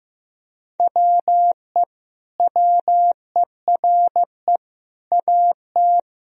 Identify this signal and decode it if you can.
Morse code, heard as a keyed tone, about 16 wpm, 700 Hz. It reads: WE WERE AT
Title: Sentences from Top 1000 Words - Review Part 1 - 15wpm